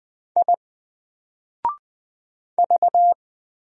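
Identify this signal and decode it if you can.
Morse code, heard as a keyed tone, about 20 wpm, 700 Hz. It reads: I V
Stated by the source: Single letter and number Morse drill at 20 wpm, from N4LCD.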